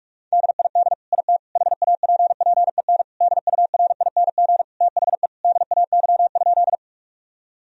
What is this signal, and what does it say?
Morse code, heard as a keyed tone, about 44 wpm, 700 Hz. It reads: DID IT HAPPEN DURING THE DAY?
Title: Sentences from Top 1000 Words - Review Part 2 - 45wpm